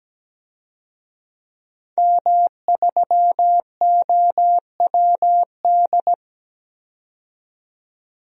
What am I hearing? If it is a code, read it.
Morse code, heard as a keyed tone, about 17 wpm, 700 Hz. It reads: M3OWD
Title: DXCC-100 Call Signs - 5 Characters - 17wpm